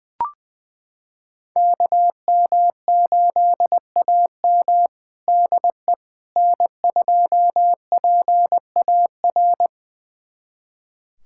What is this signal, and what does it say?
Morse code, heard as a keyed tone, about 21 wpm, 700 Hz. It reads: KM8AM DE N2PAR
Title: USA Callsign DE Callsign - 5 Characters - 20wpm